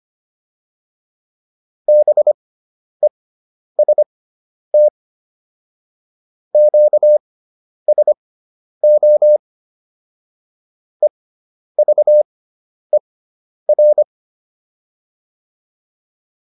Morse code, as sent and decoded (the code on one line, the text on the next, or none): -... ... - / --.- ... --- / . ...- . .-.
BEST QSO EVER